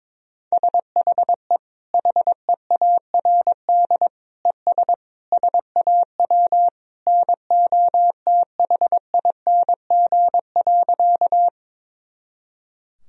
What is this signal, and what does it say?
Morse code, heard as a keyed tone, about 22 wpm, 700 Hz. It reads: SHE HEARD ES SAW NOTHING.
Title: Sentences from Top 500 Words and Abbreviations 22wpm